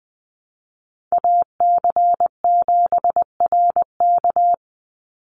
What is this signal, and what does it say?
Morse code, heard as a keyed tone, about 20 wpm, 700 Hz. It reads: AC7RK
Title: USA Call Signs 20wpm